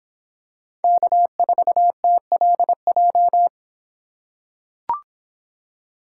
Morse code, heard as a keyed tone, about 24 wpm, 700 Hz. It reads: K4TLJ